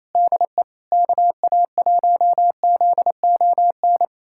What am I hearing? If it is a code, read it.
Morse code, heard as a keyed tone, about 28 wpm, 700 Hz. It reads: DE KA1ZON
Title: USA Callsign DE Callsign - 6 Characters - 28wpm